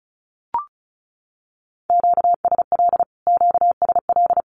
Morse code, heard as a keyed tone, about 34 wpm, 700 Hz. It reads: QSL QSL
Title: Q-Codes Repeated 3x 35wpm